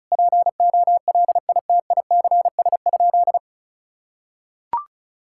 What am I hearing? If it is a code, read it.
Morse code, heard as a keyed tone, about 36 wpm, 700 Hz. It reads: POLITICS?
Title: Sentences from Top 1000 Words - Review Part 6 - 35wpm